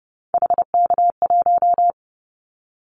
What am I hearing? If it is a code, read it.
Morse code, heard as a keyed tone, about 30 wpm, 700 Hz. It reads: HK1
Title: Callsign Prefixes for Top 100 DXCC - 30wpm